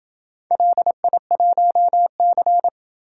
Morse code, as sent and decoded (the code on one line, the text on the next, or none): .-.. .. .---- -.-.
LI1C